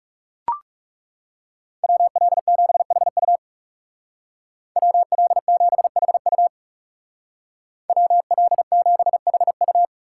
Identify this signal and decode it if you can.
Morse code, heard as a keyed tone, about 35 wpm, 700 Hz. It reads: WL7HU WL7HU WL7HU
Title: Speed Racing - USA Call Signs 30wpm